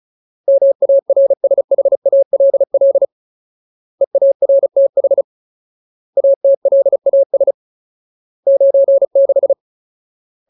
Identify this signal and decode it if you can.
Morse code, heard as a keyed tone, about 34 wpm, 550 Hz. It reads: MARSHALL EARTH ATLAS 96
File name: Ultimate Word List / 35WPM Ultimate